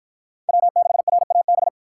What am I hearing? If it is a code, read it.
Morse code, heard as a keyed tone, about 47 wpm, 700 Hz. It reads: W6RAB